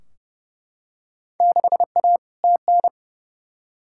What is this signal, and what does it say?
Morse code, heard as a keyed tone, about 30 wpm, 700 Hz. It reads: 6A TN
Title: Field Day Exchanges Encore - 30wpm